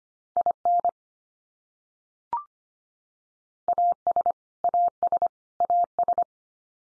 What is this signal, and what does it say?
Morse code, heard as a keyed tone, about 25 wpm, 700 Hz. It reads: IN AS AS AS
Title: Top 100 Words 25wpm x3